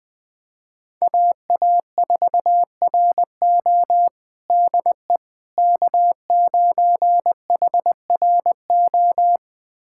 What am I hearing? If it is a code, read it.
Morse code, heard as a keyed tone, about 20 wpm, 700 Hz. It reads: AA4RO DE K9HRO
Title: USA Callsign DE Callsign - 5 Characters - 20wpm